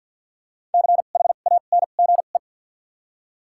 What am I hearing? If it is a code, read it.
Morse code, heard as a keyed tone, about 52 wpm, 700 Hz. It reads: C H A N G E
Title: 2x Character Spacing - Top 200 Words - 50wpm